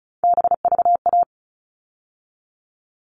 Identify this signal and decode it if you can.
Morse code, heard as a keyed tone, about 34 wpm, 700 Hz. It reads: BVA
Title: All Cut Numbers - 3-Digit 35wpm